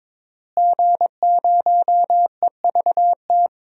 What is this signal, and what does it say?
Morse code, heard as a keyed tone, about 22 wpm, 700 Hz. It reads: G0EVT
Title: DXCC-100 Call Signs - 5 Characters - Repeated 3x - 22wpm